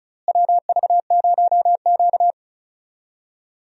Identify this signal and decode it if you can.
Morse code, heard as a keyed tone, about 36 wpm, 700 Hz. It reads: WV0Q
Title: USA Call Signs - 4 Characters - Repeated 3x - 35wpm